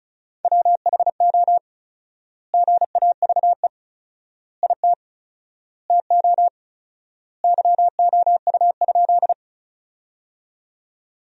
Morse code, heard as a keyed tone, about 36 wpm, 700 Hz. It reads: WHO GAVE IT TO YOU?
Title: 4x Word-Spacing - Sentences from Top 500 Words - 35wpm